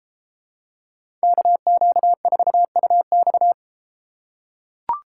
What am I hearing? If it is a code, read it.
Morse code, heard as a keyed tone, about 32 wpm, 700 Hz. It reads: KQ4UX